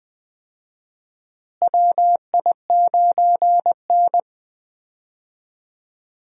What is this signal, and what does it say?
Morse code, heard as a keyed tone, about 20 wpm, 700 Hz. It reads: WI9N
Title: USA Call Signs - 4 Characters - Repeated 3x - 20wpm